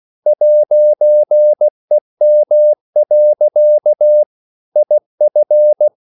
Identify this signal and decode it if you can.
Morse code, heard as a keyed tone, about 16 wpm, 600 Hz. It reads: 'EM. IF
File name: CW / Practice / EnglishSayings16